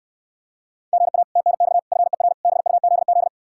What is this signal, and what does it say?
Morse code, heard as a keyed tone, about 66 wpm, 700 Hz. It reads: GA TNX FER BUZZ